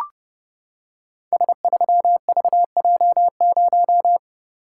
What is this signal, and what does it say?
Morse code, heard as a keyed tone, about 30 wpm, 700 Hz. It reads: S3VJ0
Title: Code Groups - 5 Letter:Numbers - 30wpm